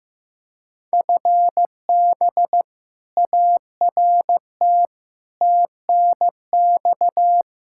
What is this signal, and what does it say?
Morse code, heard as a keyed tone, about 16 wpm, 700 Hz. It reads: FB ART TNX
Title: K1USN SST QSO Practice 15wpm